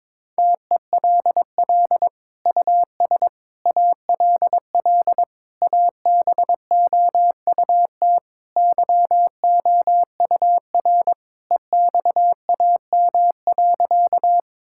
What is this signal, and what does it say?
Morse code, heard as a keyed tone, about 21 wpm, 700 Hz. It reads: TELL US ALL ABOUT YOUR EXAM.
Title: Sentences from Top 1000 Words - Review Part 5 - 22wpm